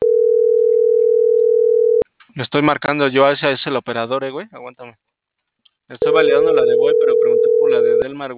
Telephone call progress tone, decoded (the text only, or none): ringback tone